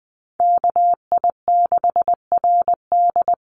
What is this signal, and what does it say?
Morse code, heard as a keyed tone, about 19 wpm, 700 Hz. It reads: KI6RD
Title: USA Call Signs Repeated 2x 20wpm